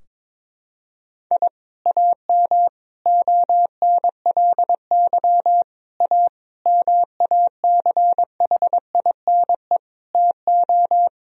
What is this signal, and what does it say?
Morse code, heard as a keyed tone, about 22 wpm, 700 Hz. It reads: I AM ONLY A MACHINE TO
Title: Sentences from Top 1000 Words - Review Part 2 - 22wpm